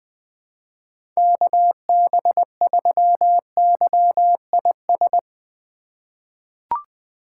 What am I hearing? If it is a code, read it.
Morse code, heard as a keyed tone, about 20 wpm, 700 Hz. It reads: KB3YIS
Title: USA Call Signs - 6 Characters - 3x Character Spacing - 20wpm